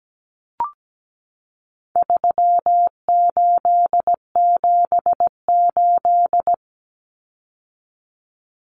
Morse code, heard as a keyed tone, about 17 wpm, 700 Hz. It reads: 3878
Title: Numbers - 4-Digit 17wpm